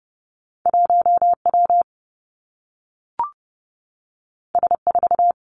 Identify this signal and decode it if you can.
Morse code, heard as a keyed tone, about 30 wpm, 700 Hz. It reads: 1 W S 4
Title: Two Challenging Letter-Number Combinations 30wpm